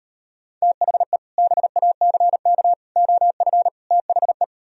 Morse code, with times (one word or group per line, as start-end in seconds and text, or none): 0.6-1.2 THE
1.4-2.7 BACK
3.0-3.7 OF
3.9-4.4 THE